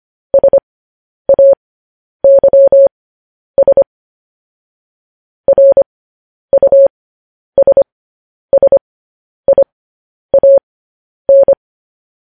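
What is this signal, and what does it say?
Morse code, heard as a keyed tone, about 25 wpm, spaced as for 10 wpm, 550 Hz. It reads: SAYS RUSSIAN
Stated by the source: Morse code news headlines broadcast, opening with vvv.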